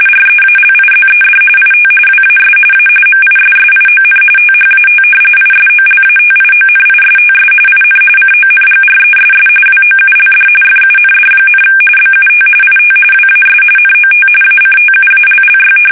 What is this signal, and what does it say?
RTTY, about 45.45 baud. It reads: REALIZED IT WAS A MISSION THAT ISOLATIONISTS IN CONGRESS WOULD FUND, BUT IT WAS NOT ACTUALLY
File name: fsk850